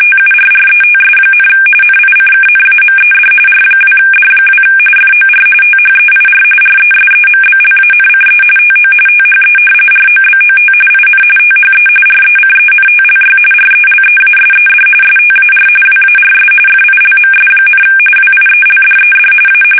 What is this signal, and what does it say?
RTTY, about 45.45 baud. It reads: TURRETS, NICKNAMED R2D2 FOR THEIR RESEMBLANCE TO THE ROBOT IN THE RECENTLY RELEASED STAR WARS FILMS, WIELDED A
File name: fsk850 / battleshipnj